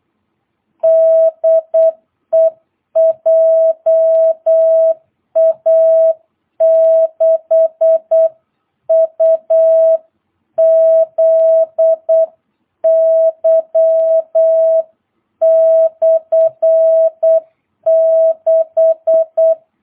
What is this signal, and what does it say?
Morse code, heard as a keyed tone, about 8 wpm, 650 Hz. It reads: DE JA6UZY/6
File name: jh8cbh / 2022cw